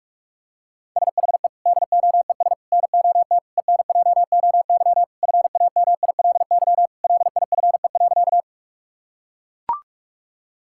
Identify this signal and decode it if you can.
Morse code, heard as a keyed tone, about 41 wpm, 700 Hz. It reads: SHE DOES NOT ENJOY FAMILY LIFE.
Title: Sentences from Top 1000 Words - Review Part 5 - 45wpm